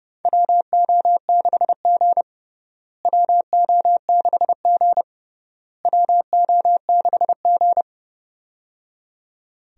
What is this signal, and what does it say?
Morse code, heard as a keyed tone, about 30 wpm, 700 Hz. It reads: WO6G WO6G WO6G
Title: USA Call Signs - 4 Characters - Repeated 3x - 30wpm